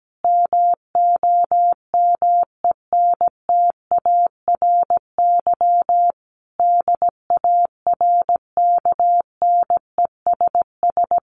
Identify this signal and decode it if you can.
Morse code, heard as a keyed tone, about 17 wpm, 700 Hz. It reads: MOMENTARY DARKNESS
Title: A Sound Of Thunder 17wpm